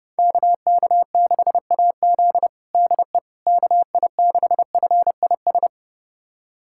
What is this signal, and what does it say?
Morse code, heard as a keyed tone, about 30 wpm, 700 Hz. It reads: KK6AZ DE KI6FIS
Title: USA Callsign DE Callsign 30wpm